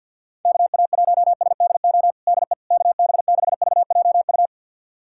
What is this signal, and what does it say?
Morse code, heard as a keyed tone, about 52 wpm, 700 Hz. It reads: KA1SDO DE KB6VJU
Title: USA Callsign DE Callsign - 6 Characters - 50wpm